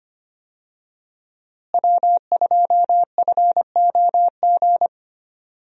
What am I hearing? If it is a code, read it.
Morse code, heard as a keyed tone, about 25 wpm, 700 Hz. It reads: W2FOG
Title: USA Call Signs - 5 Characters - Repeated 3x - 25wpm